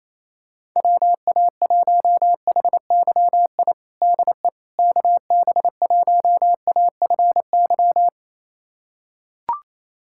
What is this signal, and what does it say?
Morse code, heard as a keyed tone, about 28 wpm, 700 Hz. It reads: WA1HYI DE KB1AFY
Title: USA Callsign DE Callsign - 6 Characters - 28wpm